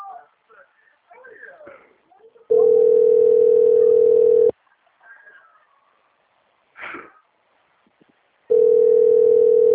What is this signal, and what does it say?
Ringback tone (North America), 440 + 480 Hz, with speech.
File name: out- - - - - .17551